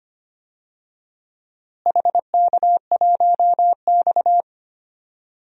Morse code, heard as a keyed tone, about 25 wpm, 700 Hz. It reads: HK1X
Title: DXCC-100 Call Signs - 4 Characters - Repeated 3x - 25wpm